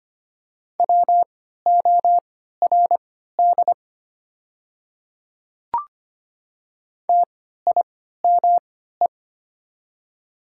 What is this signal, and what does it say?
Morse code, heard as a keyed tone, about 25 wpm, spaced as for 14 wpm, 700 Hz. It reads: WORD TIME
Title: 3x Character Spacing - Top 100 Words - 25wpm